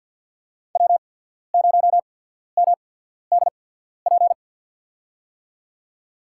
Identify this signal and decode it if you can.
Morse code, heard as a keyed tone, about 51 wpm, spaced as for 14 wpm, 700 Hz. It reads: W0MDP
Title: USA Call Signs - 5 Characters - 8x Character Spacing - 50wpm